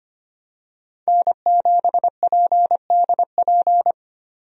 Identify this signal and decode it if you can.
Morse code, heard as a keyed tone, about 25 wpm, 700 Hz. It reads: N7PDP
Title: USA Call Signs - 5 Characters - 25wpm